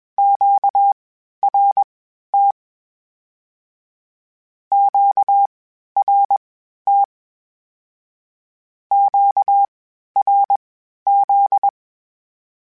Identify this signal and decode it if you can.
Morse code, heard as a keyed tone, about 21 wpm, spaced as for 12 wpm, 800 Hz. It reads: QRT QRT QRZ
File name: (spaced) / Q-Calls Spaced-mp3s / Q Calls Twice sp @20